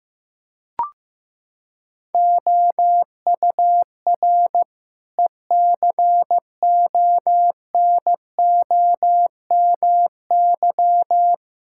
Morse code, heard as a keyed tone, about 14 wpm, 700 Hz. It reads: OUR ECONOMY
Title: Sentences from Top 1000 Words - Review Part 5 - 15wpm